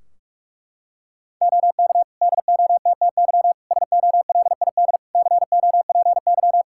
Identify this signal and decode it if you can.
Morse code, heard as a keyed tone, about 41 wpm, 700 Hz. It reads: OK DOTTY SOLID COPY
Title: K1USN SST QSO Practice 45wpm